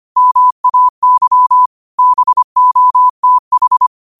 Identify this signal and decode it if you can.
Morse code, heard as a keyed tone, about 25 wpm, 1000 Hz. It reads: MAY BOTH